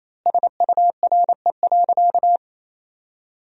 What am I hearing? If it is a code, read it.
Morse code, heard as a keyed tone, about 28 wpm, 700 Hz. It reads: SURE.
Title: 3x Word-Spacing - Sentences from Top 500 Words - 28wpm